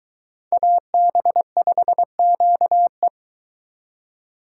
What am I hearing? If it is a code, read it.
Morse code, heard as a keyed tone, about 22 wpm, 700 Hz. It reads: AB5QE